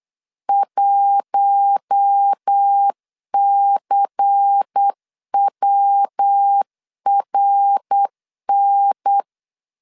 Morse code, heard as a keyed tone, about 8 wpm, 800 Hz. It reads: 1CWRN
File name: LEZIONI / 8 YJ45 / 8 Y 8wpm 3245